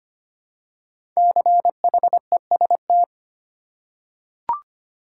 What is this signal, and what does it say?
Morse code, heard as a keyed tone, about 25 wpm, 700 Hz. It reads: CHEST
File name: 5 Letter Words - First 3 Letters Given - 25wpm